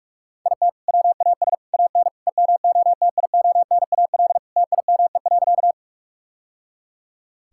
Morse code, heard as a keyed tone, about 45 wpm, 700 Hz. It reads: IT WAS AN EMOTIONAL TIME.
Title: Sentences from Top 1000 Words - Review Part 5 - 45wpm